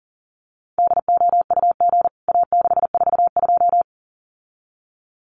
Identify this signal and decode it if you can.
Morse code, heard as a keyed tone, about 40 wpm, 700 Hz. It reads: DOUG A642
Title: CWT Exchange 40wpm